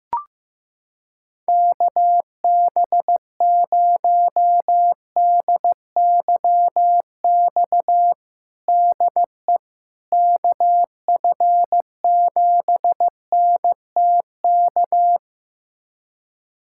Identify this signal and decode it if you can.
Morse code, heard as a keyed tone, about 15 wpm, 700 Hz. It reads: KB0DYX DE KF7NTK
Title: USA Callsign DE Callsign - 6 Characters - 15wpm